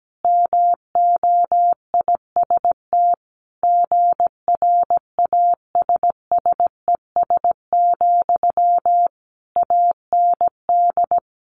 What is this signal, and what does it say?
Morse code, heard as a keyed tone, about 17 wpm, 700 Hz. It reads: MOIST GRASSES, AND
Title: A Sound Of Thunder 17wpm